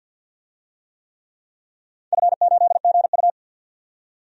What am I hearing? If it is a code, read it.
Morse code, heard as a keyed tone, about 52 wpm, 700 Hz. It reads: F8GU
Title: DXCC-100 Call Signs - 4 Characters - 50wpm